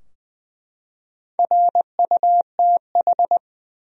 Morse code, heard as a keyed tone, about 20 wpm, 700 Hz. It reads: RUTH
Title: USA Common Names - Max 4 Letters - 20wpm